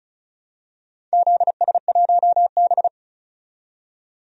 Morse code, heard as a keyed tone, about 36 wpm, 700 Hz. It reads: ZS1B
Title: DXCC-100 Call Signs - 4 Characters - 35wpm